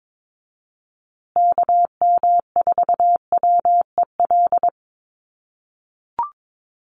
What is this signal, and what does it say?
Morse code, heard as a keyed tone, about 22 wpm, 700 Hz. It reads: KM4WEL
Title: USA Call Signs Repeated 2x 22wpm